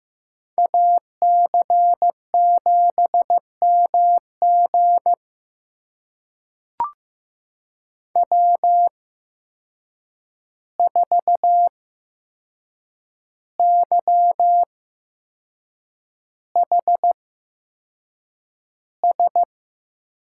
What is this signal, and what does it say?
Morse code, heard as a keyed tone, about 15 wpm, spaced as for 6 wpm, 700 Hz. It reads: AC7MG W4YHS
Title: USA Call Signs - 8x Character Spacing - 15wpm